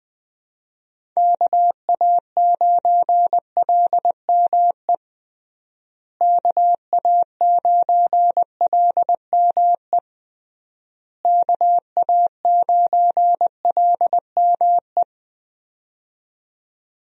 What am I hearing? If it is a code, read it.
Morse code, heard as a keyed tone, about 20 wpm, 700 Hz. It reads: KA9LME KA9LME KA9LME
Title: USA Call Signs - 6 Characters - Repeated 3x - 20wpm